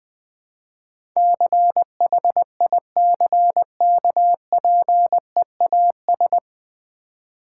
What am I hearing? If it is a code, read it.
Morse code, heard as a keyed tone, about 21 wpm, 700 Hz. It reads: CHICKPEAS
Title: Difficult Words with Hints - 20wpm